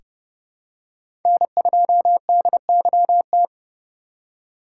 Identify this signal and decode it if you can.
Morse code, heard as a keyed tone, about 32 wpm, 700 Hz. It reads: N2DYT
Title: K1USN SST QSO Practice - 30wpm